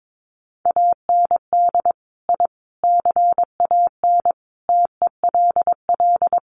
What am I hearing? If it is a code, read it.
Morse code, heard as a keyed tone, about 22 wpm, 700 Hz. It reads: AND I CAN TELL
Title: Sets of 5 Words Encore2 22wpm